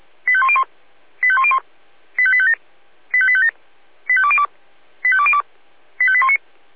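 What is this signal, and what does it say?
Five-tone selective call (ZVEI), heard as quick runs of five tones, tones of about 70 ms. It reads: 85101 85101 85857 85857 85202 85202 85819
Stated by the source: ZVEI five-tone selective call 85810 received on an SDR.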